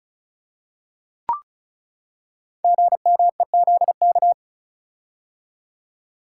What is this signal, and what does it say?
Morse code, heard as a keyed tone, about 36 wpm, 700 Hz. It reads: GMEZK
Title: Code Groups - 5 Letter:Numbers - 35wpm